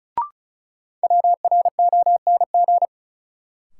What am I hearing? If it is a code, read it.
Morse code, heard as a keyed tone, about 36 wpm, 700 Hz. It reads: WRONG
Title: Sets of 5 Words Encore - New Word Drill - 35wpm